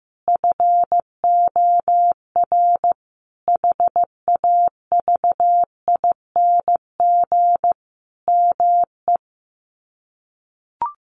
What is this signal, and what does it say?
Morse code, heard as a keyed tone, about 14 wpm, 700 Hz. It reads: FOR HAVING ME
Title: Sets of 5 Words Encore 15wpm